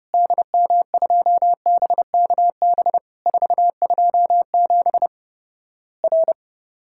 Morse code, heard as a keyed tone, about 29 wpm, 700 Hz. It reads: DM2BKB 427 R
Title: Worked All Europe DX Contest - QTC Exchange - 30wpm